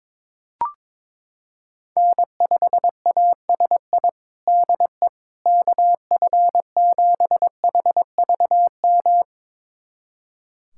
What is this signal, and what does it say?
Morse code, heard as a keyed tone, about 21 wpm, 700 Hz. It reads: N5ASI DE KF7HVM
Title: USA Callsign DE Callsign 22wpm